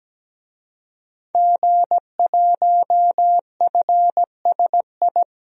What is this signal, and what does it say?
Morse code, heard as a keyed tone, about 17 wpm, 700 Hz. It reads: G1FSI
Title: DXCC-100 Call Signs - 5 Characters - Repeated 2x - 17wpm